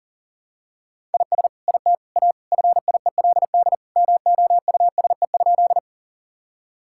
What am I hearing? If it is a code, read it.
Morse code, heard as a keyed tone, about 40 wpm, 700 Hz. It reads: IS IT A FIELD MOUSE?
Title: Sentences from Top 1000 Words - Review Part 5 - 40wpm